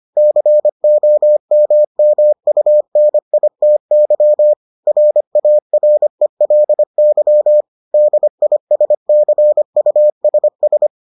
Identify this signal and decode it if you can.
Morse code, heard as a keyed tone, about 25 wpm, 600 Hz. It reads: COMMUNITY RARELY DISCUSS